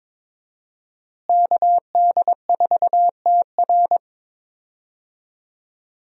Morse code, heard as a keyed tone, about 22 wpm, 700 Hz. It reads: KD4TR
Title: USA Call Signs - 5 Characters - Repeated 3x - 22wpm